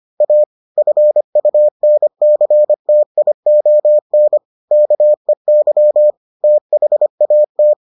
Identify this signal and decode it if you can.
Morse code, heard as a keyed tone, about 25 wpm, 600 Hz. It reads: A FUNCTION KEY THAT